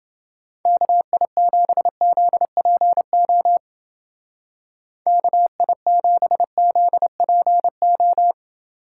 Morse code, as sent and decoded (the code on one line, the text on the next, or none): -.- .. --... --.. .--. --- / -.- .. --... --.. .--. ---
KI7ZPO KI7ZPO